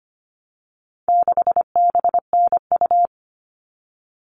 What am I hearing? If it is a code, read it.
Morse code, heard as a keyed tone, about 25 wpm, 700 Hz. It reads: 6BNU
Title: All Cut Numbers - 4-Digit 25wpm